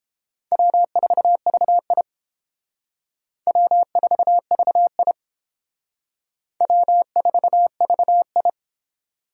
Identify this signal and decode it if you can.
Morse code, heard as a keyed tone, about 28 wpm, 700 Hz. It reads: W4VI W4VI W4VI